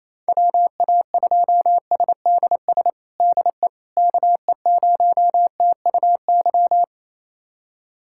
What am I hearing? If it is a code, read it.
Morse code, heard as a keyed tone, about 28 wpm, 700 Hz. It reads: WA2SDS DE KE0TUY
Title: USA Callsign DE Callsign - 6 Characters - 28wpm